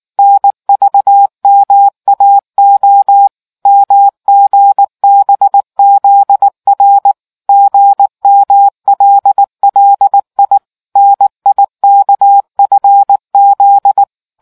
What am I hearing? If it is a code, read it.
Morse code, heard as a keyed tone, about 19 wpm, 800 Hz. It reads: NVMAO MGBZR GMLLI NIKFZ